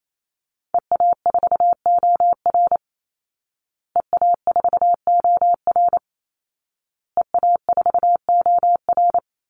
Morse code, heard as a keyed tone, about 28 wpm, 700 Hz. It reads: EA4OR EA4OR EA4OR
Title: Call Signs Repeated 2x - 28wpm